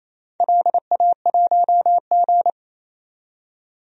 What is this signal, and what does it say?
Morse code, heard as a keyed tone, about 29 wpm, 700 Hz. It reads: LA1G
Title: DXCC-100 Call Signs - 4 Characters - 28wpm